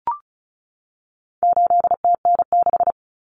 Morse code, heard as a keyed tone, about 34 wpm, 700 Hz. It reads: 8TN6
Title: Common Cut Numbers - 4-Digit 35wpm